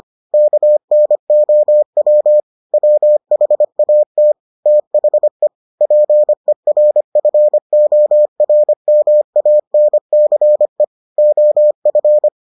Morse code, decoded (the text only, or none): KNOW WHAT THE PERFORMANCE OF